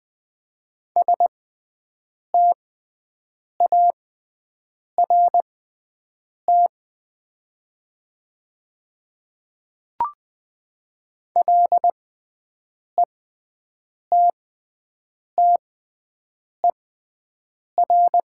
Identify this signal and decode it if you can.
Morse code, heard as a keyed tone, about 20 wpm, spaced as for 7 wpm, 700 Hz. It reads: START LETTER
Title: 6x Character Spacing - Top 300 Words - 20wpm